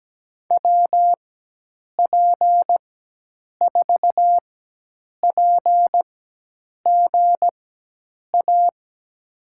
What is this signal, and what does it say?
Morse code, heard as a keyed tone, about 17 wpm, spaced as for 8 wpm, 700 Hz. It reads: WP4PGA
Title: USA Call Signs - 6 Characters - 4x Character Spacing - 17wpm